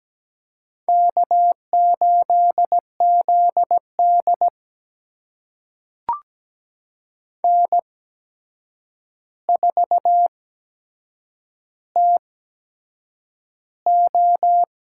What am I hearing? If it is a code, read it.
Morse code, heard as a keyed tone, about 17 wpm, spaced as for 7 wpm, 700 Hz. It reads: K8ZD N4TO
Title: USA Call Signs - 4 Characters - 8x Character Spacing - 17wpm